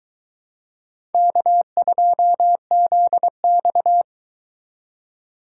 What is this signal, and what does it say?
Morse code, heard as a keyed tone, about 22 wpm, 700 Hz. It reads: K2ZX